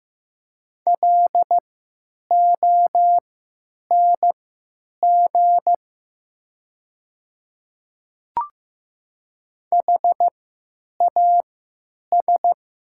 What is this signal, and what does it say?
Morse code, heard as a keyed tone, about 15 wpm, spaced as for 9 wpm, 700 Hz. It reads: LONG HAS
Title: 3x Character Spacing - Top 500 Words - 15wpm